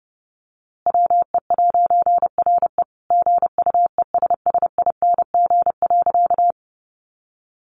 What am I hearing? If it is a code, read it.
Morse code, heard as a keyed tone, about 30 wpm, 700 Hz. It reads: WE'RE GUESSING.
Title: A Sound Of Thunder 30wpm